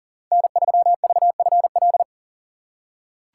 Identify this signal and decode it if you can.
Morse code, heard as a keyed tone, about 40 wpm, 700 Hz. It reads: N3VFL
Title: USA Call Signs - 5 Characters - 2x Character Spacing - 40wpm